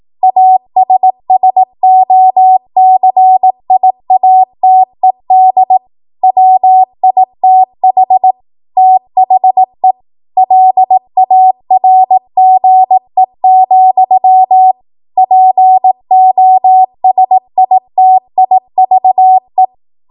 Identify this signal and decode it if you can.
Morse code, heard as a keyed tone, about 18 wpm, 750 Hz. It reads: ASSOCIATED WITH THE LARGE, POSITIVE
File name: Morse / 18WPM